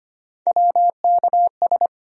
Morse code, heard as a keyed tone, about 25 wpm, 700 Hz. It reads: WKS
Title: USA Call Sign Postfixes - 3 Letters 25wpm